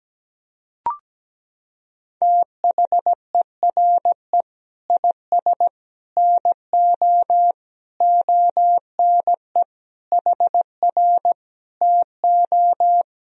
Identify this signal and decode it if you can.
Morse code, heard as a keyed tone, about 17 wpm, 700 Hz. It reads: THERE IS NO ONE HR TO
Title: Sentences from Top 200 Words and Abbreviations 17wpm